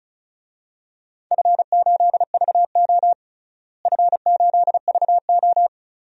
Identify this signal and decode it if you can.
Morse code, heard as a keyed tone, about 34 wpm, 700 Hz. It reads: F8VO F8VO
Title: DXCC-100 Call Signs - 4 Characters - Repeated 3x - 35wpm